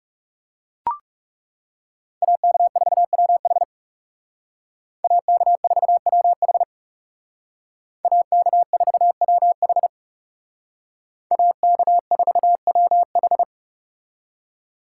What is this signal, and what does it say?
Morse code, heard as a keyed tone, about 38 wpm, 700 Hz. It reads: AK4WH AK4WH AK4WH AK4WH